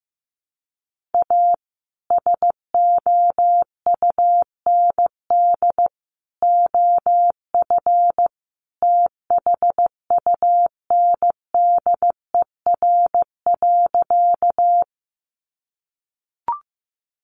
Morse code, heard as a keyed tone, about 15 wpm, 700 Hz. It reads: A SOUND OF THUNDER.
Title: A Sound Of Thunder 15wpm